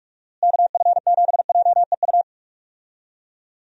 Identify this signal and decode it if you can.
Morse code, heard as a keyed tone, about 46 wpm, 700 Hz. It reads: KF7JEU
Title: USA Call Signs - 6 Characters - Repeated 3x - 45wpm